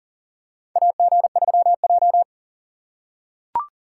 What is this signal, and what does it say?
Morse code, heard as a keyed tone, about 40 wpm, 700 Hz. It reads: AG3J
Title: USA Call Signs - 4 Characters - 6x Character Spacing - 40wpm